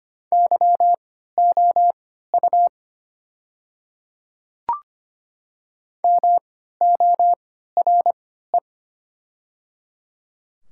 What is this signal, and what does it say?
Morse code, heard as a keyed tone, about 25 wpm, spaced as for 14 wpm, 700 Hz. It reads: YOU MORE
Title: 3x Character Spacing - Top 200 Words - 25wpm